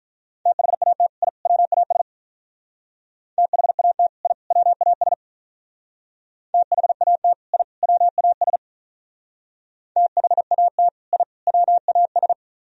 Morse code, heard as a keyed tone, about 46 wpm, 700 Hz. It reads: THAT I WAS THAT I WAS THAT I WAS THAT I WAS